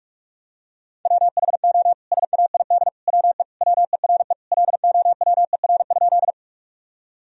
Morse code, heard as a keyed tone, about 46 wpm, 700 Hz. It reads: WHO SAID WE WERE LOWER?
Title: Sentences from Top 1000 Words - Review Part 4 - 45wpm